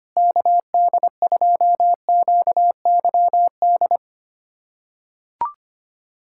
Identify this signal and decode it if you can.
Morse code, heard as a keyed tone, about 25 wpm, 700 Hz. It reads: KD2QYD
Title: USA Callsign DE Callsign - 6 Characters - 25wpm